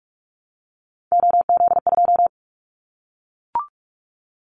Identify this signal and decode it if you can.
Morse code, heard as a keyed tone, about 44 wpm, 700 Hz. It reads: OZ2
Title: Callsign Prefixes for Top 100 DXCC - 45wpm